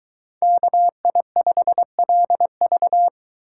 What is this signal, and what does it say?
Morse code, heard as a keyed tone, about 23 wpm, 700 Hz. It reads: KI5LV